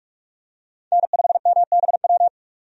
Morse code, heard as a keyed tone, about 46 wpm, 700 Hz. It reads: N5MBW
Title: USA Call Signs - 5 Characters - 45wpm